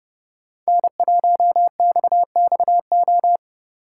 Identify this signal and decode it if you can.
Morse code, heard as a keyed tone, about 30 wpm, 700 Hz. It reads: N1XXO
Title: USA Call Signs - 5 Characters - 4x Character Spacing - 30wpm